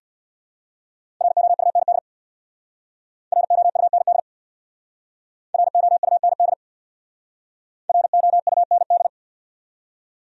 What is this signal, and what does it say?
Morse code, heard as a keyed tone, about 65 wpm, 700 Hz. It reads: ROUND ROUND ROUND ROUND